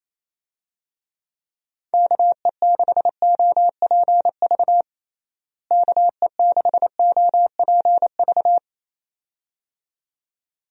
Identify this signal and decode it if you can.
Morse code, heard as a keyed tone, about 28 wpm, 700 Hz. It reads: KE6OPV KE6OPV
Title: USA Call Signs - 6 Characters - Repeated 2x - 28wpm